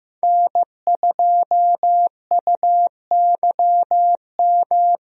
Morse code, heard as a keyed tone, about 15 wpm, 700 Hz. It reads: N2UYM
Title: USA Call Signs - 5 Characters - Repeated 2x - 15wpm